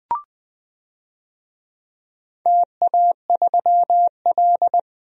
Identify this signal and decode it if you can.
Morse code, heard as a keyed tone, about 20 wpm, 700 Hz. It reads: TA3L
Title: DXCC-100 Call Signs - 4 Characters - 20wpm